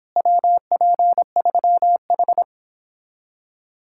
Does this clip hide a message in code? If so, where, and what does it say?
Yes: Morse code from 0.2 to 2.4 s, heard as a keyed tone, about 31 wpm, 700 Hz. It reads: WP3H